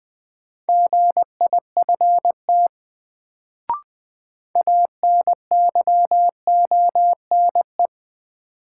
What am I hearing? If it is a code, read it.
Morse code, heard as a keyed tone, about 20 wpm, 700 Hz. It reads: GIFT ANYONE
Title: Sentences from Top 900 Words - New Word Drill - 20wpm